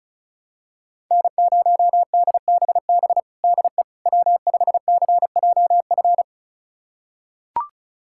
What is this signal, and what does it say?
Morse code, heard as a keyed tone, about 36 wpm, 700 Hz. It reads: N0DBB DE W5CJF
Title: USA Callsign DE Callsign - 5 Characters - 35wpm